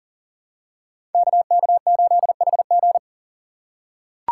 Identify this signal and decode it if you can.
Morse code, heard as a keyed tone, about 39 wpm, 700 Hz. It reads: KK8HG
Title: Code Groups - 5 Letter:Numbers - 40wpm